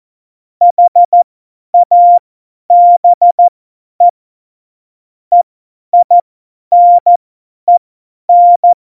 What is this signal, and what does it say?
Morse code, heard as a keyed tone, about 14 wpm, spaced as for 10 wpm, 700 Hz. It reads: HABE EINEN
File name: Morsekurs / Witze / Witz 08. Dezember 2020, 50 F70